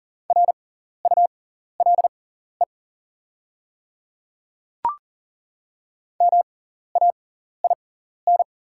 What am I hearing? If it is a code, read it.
Morse code, heard as a keyed tone, about 40 wpm, spaced as for 14 wpm, 690 Hz. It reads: RULE MAIN
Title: 6x Character Spacing - Top 500 Words - 40wpm